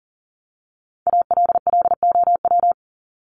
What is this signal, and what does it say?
Morse code, heard as a keyed tone, about 40 wpm, 700 Hz. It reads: ALLOW
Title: Top 501-750 Words 40wpm